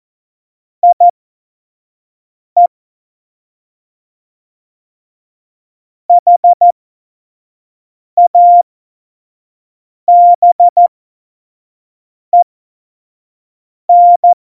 Morse code, heard as a keyed tone, about 14 wpm, spaced as for 5 wpm, 700 Hz. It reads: IE HABEN
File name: Morsekurs / Witze / Witz 29. Dezember 2020, 25 F70